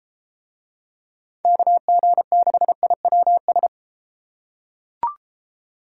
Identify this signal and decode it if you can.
Morse code, heard as a keyed tone, about 31 wpm, 700 Hz. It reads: KG6IWS